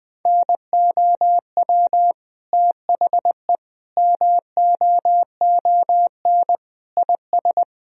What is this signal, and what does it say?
Morse code, heard as a keyed tone, about 20 wpm, 700 Hz. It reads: NOW THE MOON IS
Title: Sentences from Top 1000 Words - Review Part 3 - 20wpm